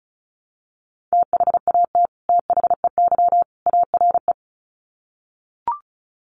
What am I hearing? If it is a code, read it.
Morse code, heard as a keyed tone, about 34 wpm, 700 Hz. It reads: THAT THEY ARE
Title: Sets of 3 Words 35wpm